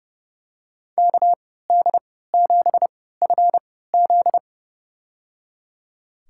Morse code, heard as a keyed tone, about 32 wpm, spaced as for 17 wpm, 700 Hz. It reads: KD7FZ
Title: USA Call Signs - 3x Character Spacing - 30wpm